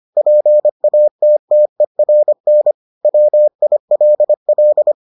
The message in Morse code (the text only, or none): PATTERN WILL